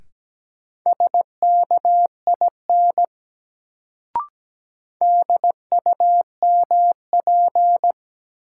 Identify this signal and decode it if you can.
Morse code, heard as a keyed tone, about 17 wpm, 700 Hz. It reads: SKIN DUMP
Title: Sentences from Top 1300 Words - New Word Drill Alt - 17wpm